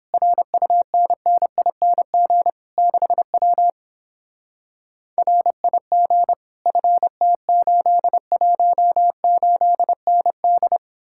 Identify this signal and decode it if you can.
Morse code, heard as a keyed tone, about 31 wpm, 700 Hz. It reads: RUNNING 6W RIG FT818ND